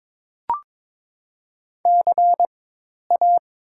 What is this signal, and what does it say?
Morse code, heard as a keyed tone, about 23 wpm, spaced as for 10 wpm, 700 Hz. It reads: CA
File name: Top 90 QSO Elements - 4x Character Spacing - 22wpm